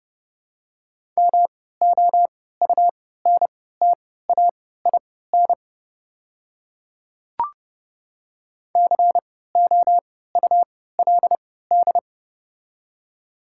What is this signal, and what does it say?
Morse code, heard as a keyed tone, about 31 wpm, spaced as for 17 wpm, 700 Hz. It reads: MOUNTAIN COULD
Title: 3x Character Spacing - Top 300 Words - 30wpm